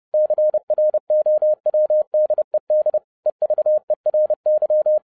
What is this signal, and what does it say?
Morse code, heard as a keyed tone, about 30 wpm, 610 Hz. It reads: CROWDED EVERY
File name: 2012-01-mp3 / n12-I DIDNT MISS THE SOLAR CYCLE 24-30wpm